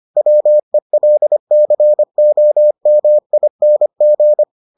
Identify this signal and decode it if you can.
Morse code, heard as a keyed tone, about 25 wpm, 600 Hz. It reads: WELCOMING